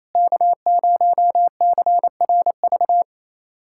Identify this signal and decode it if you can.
Morse code, heard as a keyed tone, about 28 wpm, 700 Hz. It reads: K0CRV
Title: USA Callsign DE Callsign 28wpm